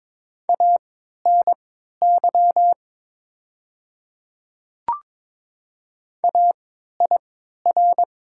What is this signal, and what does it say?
Morse code, heard as a keyed tone, about 22 wpm, spaced as for 12 wpm, 700 Hz. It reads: ANY AIR
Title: 3x Character Spacing - Top 200 Words - 22wpm